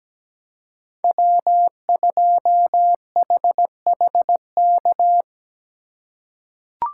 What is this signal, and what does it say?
Morse code, heard as a keyed tone, about 17 wpm, 700 Hz. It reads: W2HHK
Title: USA Call Signs - 5 Characters - 4x Character Spacing - 17wpm